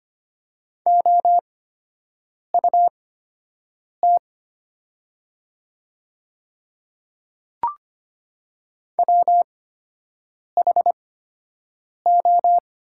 Morse code, heard as a keyed tone, about 25 wpm, spaced as for 7 wpm, 700 Hz. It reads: OUT WHO
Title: 8x Character Spacing - Top 100 Words - 25wpm